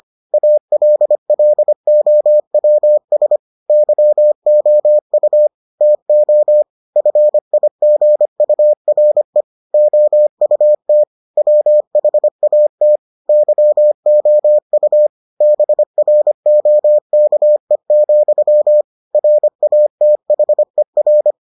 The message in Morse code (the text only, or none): ALLOWS YOU TO FIGURE OUT WHAT YOU BROKE, RATHER